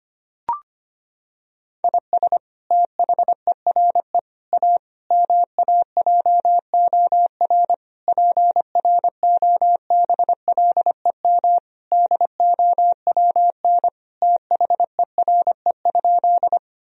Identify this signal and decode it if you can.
Morse code, heard as a keyed tone, about 25 wpm, 700 Hz. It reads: IS THERE A MAJOR PROBLEM DOWN THERE?
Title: Sentences from Top 1000 Words - Review Part 4 - 25wpm